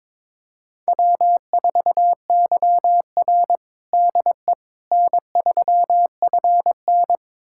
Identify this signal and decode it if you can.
Morse code, heard as a keyed tone, about 22 wpm, 700 Hz. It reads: W4YR DE N3FN
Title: USA Callsign DE Callsign - 4 Characters - 22wpm